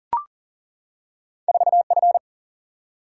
Morse code, heard as a keyed tone, about 39 wpm, 700 Hz. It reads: 4 F
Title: Single-Letter Single-Number 40wpm